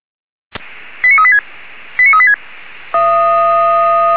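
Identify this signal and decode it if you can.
Five-tone selective call (ZVEI), heard as quick runs of five tones, tones of about 70 ms. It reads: 89386 89386
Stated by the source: Received by SDR; ZVEI five-tone code 89185.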